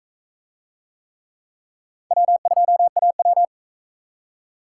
Morse code, heard as a keyed tone, about 40 wpm, 700 Hz. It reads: W2AW